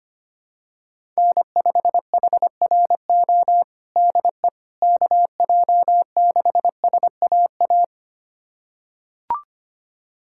Morse code, heard as a keyed tone, about 25 wpm, 700 Hz. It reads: N5HRO DE KJ6SAA
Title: USA Callsign DE Callsign 25wpm